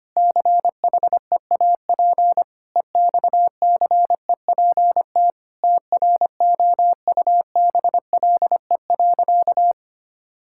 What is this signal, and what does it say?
Morse code, heard as a keyed tone, about 25 wpm, 700 Hz. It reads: CHEAP EXCEPT TROUBLE.
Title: Sentences from Top 1600 Words 25wpm